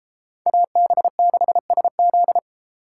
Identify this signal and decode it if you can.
Morse code, heard as a keyed tone, about 30 wpm, 700 Hz. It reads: AB6SZ